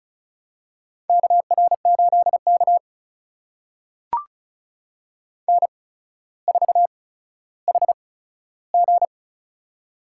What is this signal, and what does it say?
Morse code, heard as a keyed tone, about 36 wpm, spaced as for 15 wpm, 700 Hz. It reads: KR8K N4HG
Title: USA Call Signs - 4 Characters - 8x Character Spacing - 35wpm